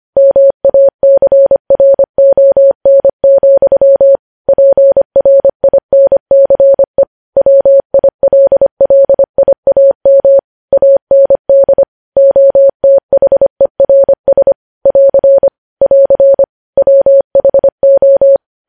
Morse code, heard as a keyed tone, about 25 wpm, 550 Hz. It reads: MACRON, PRINCE WILLIAM AND OTHERS <AR> <AR> WHO
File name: podcast / News headlines-1x-25x25